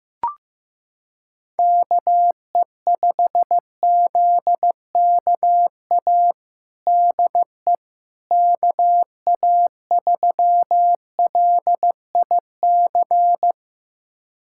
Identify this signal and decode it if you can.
Morse code, heard as a keyed tone, about 15 wpm, 700 Hz. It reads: KE5ZKA DE KA3LIC
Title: USA Callsign DE Callsign 15wpm